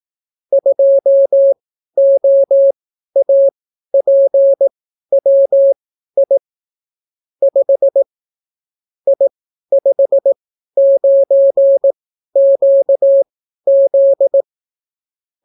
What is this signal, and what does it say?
Morse code, heard as a keyed tone, about 18 wpm, spaced as for 12 wpm, 550 Hz. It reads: 2OAPWI 5 I59QZ